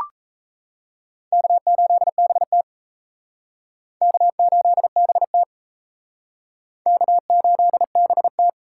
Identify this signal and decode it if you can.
Morse code, heard as a keyed tone, about 36 wpm, 700 Hz. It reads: K8BT K8BT K8BT